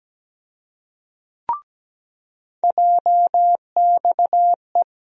Morse code, heard as a keyed tone, about 17 wpm, 700 Hz. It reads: JXE
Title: USA Call Sign Postfixes - 3 Letters 17wpm